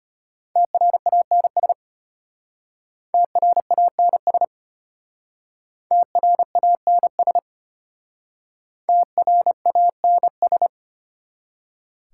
Morse code, heard as a keyed tone, about 32 wpm, 700 Hz. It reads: TRANS TRANS TRANS TRANS